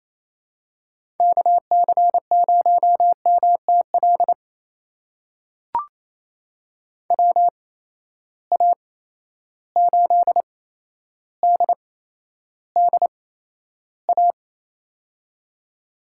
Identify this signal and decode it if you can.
Morse code, heard as a keyed tone, about 28 wpm, spaced as for 12 wpm, 700 Hz. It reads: KC0MTL WA8DDA
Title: USA Call Signs - 6 Characters - 8x Character Spacing - 28wpm